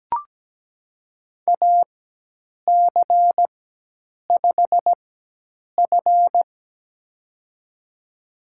Morse code, heard as a keyed tone, about 17 wpm, spaced as for 8 wpm, 700 Hz. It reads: AC5F
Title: USA Call Signs - 4 Characters - 4x Character Spacing - 17wpm